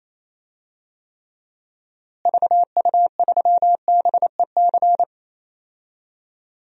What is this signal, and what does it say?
Morse code, heard as a keyed tone, about 27 wpm, 700 Hz. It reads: VU3BEC
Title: DXCC-100 Call Signs - 6 Characters - Repeated 2x - 28wpm